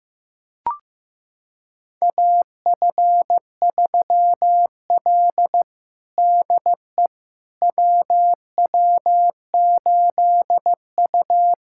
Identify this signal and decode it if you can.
Morse code, heard as a keyed tone, about 15 wpm, 700 Hz. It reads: AF3L DE WW8U
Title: USA Callsign DE Callsign - 4 Characters - 15wpm